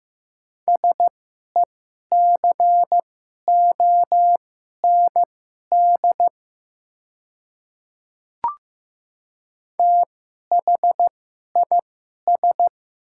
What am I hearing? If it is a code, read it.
Morse code, heard as a keyed tone, about 15 wpm, spaced as for 11 wpm, 700 Hz. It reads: SECOND THIS
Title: 2x Character Spacing - Top 300 Words - 15wpm